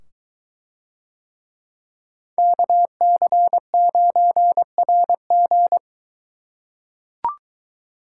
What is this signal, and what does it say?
Morse code, heard as a keyed tone, about 24 wpm, 700 Hz. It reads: KC9RG